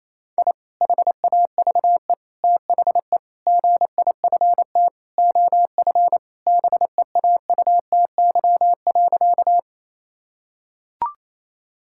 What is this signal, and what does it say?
Morse code, heard as a keyed tone, about 28 wpm, 700 Hz. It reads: I HAVE THE GIFT OF BEAUTY.
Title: Sentences from Top 1000 Words - Review Part 6 - 28wpm